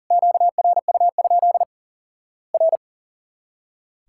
Morse code, heard as a keyed tone, about 40 wpm, 700 Hz. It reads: QRU? R
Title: Worked All Europe DX Contest - QTC Exchange - 40wpm